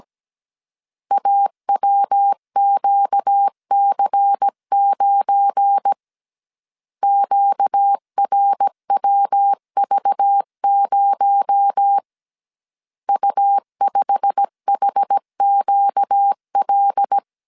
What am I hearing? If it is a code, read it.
Morse code, heard as a keyed tone, about 17 wpm, 800 Hz. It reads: AWQC9 QRWV0 U5HQL
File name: CW / DET / 1 9 1500C S A 16wpm 2828